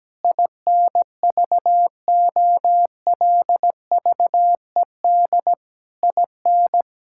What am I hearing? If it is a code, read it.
Morse code, heard as a keyed tone, about 17 wpm, 700 Hz. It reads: INVOLVED IN
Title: Sentences from Top 1000 Words - Review Part 5 - 17wpm